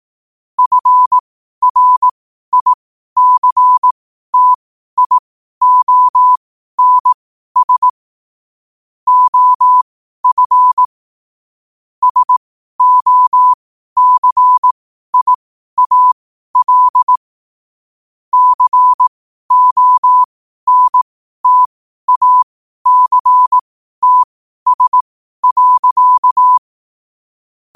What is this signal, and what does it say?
Morse code, heard as a keyed tone, about 18 wpm, spaced as for 12 wpm, 1000 Hz. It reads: FRICTIONS OF SOCIAL CONTACTS.
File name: qotdfiles / 12wpm